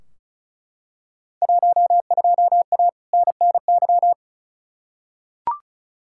Morse code, heard as a keyed tone, about 36 wpm, 700 Hz. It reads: 12A NNY
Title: Field Day Exchanges Encore - 35wpm